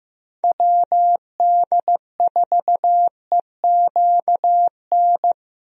Morse code, heard as a keyed tone, about 16 wpm, 700 Hz. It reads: WD4EQN
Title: USA Call Signs - 6 Characters - Repeated 3x - 15wpm